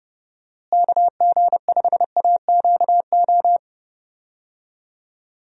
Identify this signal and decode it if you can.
Morse code, heard as a keyed tone, about 30 wpm, 700 Hz. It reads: KG5AQO
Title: USA Call Signs - 6 Characters - Repeated 3x - 30wpm